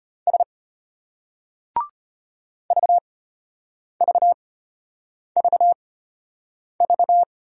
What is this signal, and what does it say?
Morse code, heard as a keyed tone, about 33 wpm, spaced as for 8 wpm, 700 Hz. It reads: S VVVV